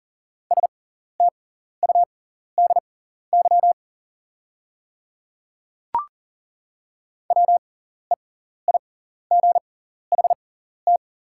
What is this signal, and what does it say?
Morse code, heard as a keyed tone, about 40 wpm, spaced as for 14 wpm, 690 Hz. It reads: STUDY WEIGHT
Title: 6x Character Spacing - Top 500 Words - 40wpm